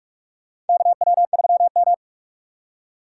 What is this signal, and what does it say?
Morse code, heard as a keyed tone, about 46 wpm, 700 Hz. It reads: KW3M
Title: USA Call Signs - 2x Character Spacing - 45wpm